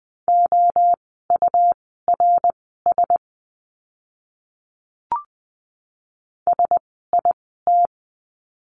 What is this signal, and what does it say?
Morse code, heard as a keyed tone, about 20 wpm, spaced as for 14 wpm, 700 Hz. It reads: OURS SIT
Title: 101 Lesson 17f - Letter u - Words 20wpm